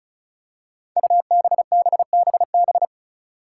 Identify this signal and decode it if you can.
Morse code, heard as a keyed tone, about 36 wpm, 700 Hz. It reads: UBBBB
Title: Five Challenging Letter-Number Combinations 35wpm